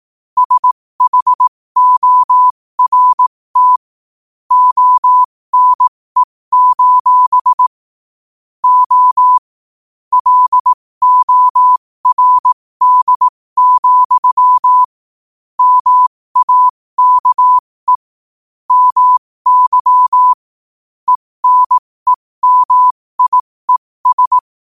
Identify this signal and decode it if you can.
Morse code, heard as a keyed tone, about 18 wpm, spaced as for 15 wpm, 1000 Hz. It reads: SHORT ONE: O LORD, MAKE MY ENEMIES
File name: qotdfiles / 15wpm